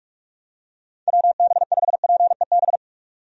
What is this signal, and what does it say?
Morse code, heard as a keyed tone, about 47 wpm, 700 Hz. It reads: WB5PEB
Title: USA Call Signs - 6 Characters - 45wpm